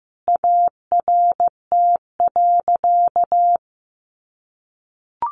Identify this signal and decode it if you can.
Morse code, heard as a keyed tone, about 15 wpm, 700 Hz. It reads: ART.
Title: Sentences from Top 1200 Words 15wpm